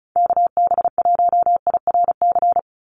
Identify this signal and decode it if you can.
Morse code, heard as a keyed tone, about 34 wpm, 700 Hz. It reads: KB1IRC
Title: USA Call Signs 35wpm